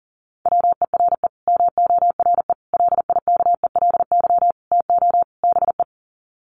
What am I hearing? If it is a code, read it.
Morse code, heard as a keyed tone, about 40 wpm, 700 Hz. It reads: WERE MORE LIKELY TO BE
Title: Sets of 5 Words Encore2 40wpm